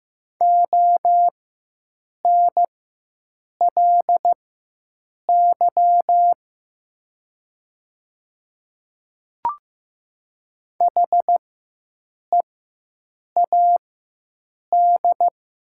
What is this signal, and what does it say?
Morse code, heard as a keyed tone, about 15 wpm, spaced as for 7 wpm, 700 Hz. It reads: ONLY HEAD
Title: 4x Character Spacing - Top 300 Words - 15wpm